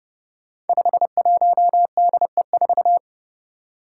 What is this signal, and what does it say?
Morse code, heard as a keyed tone, about 30 wpm, 700 Hz. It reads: 51DE4
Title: Five Challenging Letter-Number Combinations 30wpm